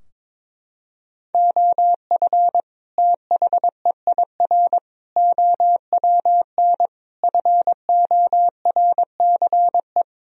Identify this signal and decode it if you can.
Morse code, heard as a keyed tone, about 22 wpm, 700 Hz. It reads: OF THEIR OWN FORCE
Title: Sentences from Top 1000 Words - Review Part 3 - 22wpm